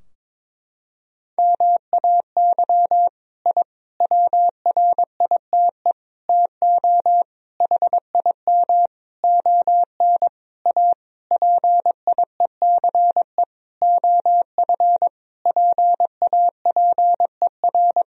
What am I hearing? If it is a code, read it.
Morse code, heard as a keyed tone, about 22 wpm, 700 Hz. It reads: MAY I WRITE TO HIM ON A PIECE OF PAPER
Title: Sentences from Top 1000 Words - Review Part 2 - 22wpm